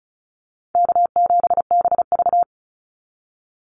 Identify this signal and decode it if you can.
Morse code, heard as a keyed tone, about 34 wpm, 700 Hz. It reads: K7BV
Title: Easy Call Signs Repeated 3x - 35wpm